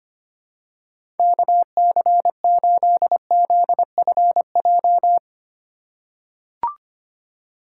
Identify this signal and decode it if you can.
Morse code, heard as a keyed tone, about 25 wpm, 700 Hz. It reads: KC8ZFJ
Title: USA Call Signs - 4x Character Spacing - 25wpm